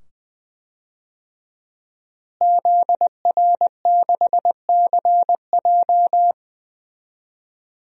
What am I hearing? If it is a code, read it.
Morse code, heard as a keyed tone, about 21 wpm, 700 Hz. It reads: ZR6CJ
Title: DXCC-100 Call Signs - 5 Characters - Repeated 2x - 20wpm